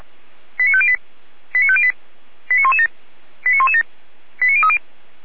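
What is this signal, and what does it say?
Five-tone selective call (ZVEI), heard as quick runs of five tones, tones of about 70 ms. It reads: 89508 89508 89117 89117 89030
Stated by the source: ZVEI five-tone selective call received on an SDR.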